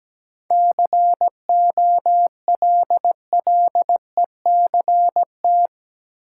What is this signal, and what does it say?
Morse code, heard as a keyed tone, about 17 wpm, 700 Hz. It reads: COLLECT
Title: Sentences from Top 1600 Words - New Word Drill Alt - 17wpm